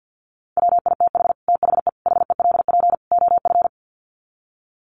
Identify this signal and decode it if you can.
Morse code, heard as a keyed tone, about 50 wpm, 700 Hz. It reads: WITH THE HELP OF
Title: Sets of 4 Words 50wpm